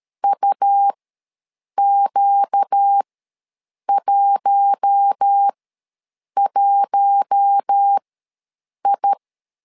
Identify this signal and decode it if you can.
Morse code, heard as a keyed tone, about 13 wpm, spaced as for 7 wpm, 800 Hz. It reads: UQ11I
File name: LEZIONI / 4 Q1 ws 12wpm 2129